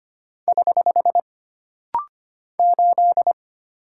Morse code, heard as a keyed tone, about 25 wpm, 700 Hz. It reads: <HH> 8
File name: Single Character (Letter:Number:Punctuation:Prosign) - Rapid-Fire - 25wpm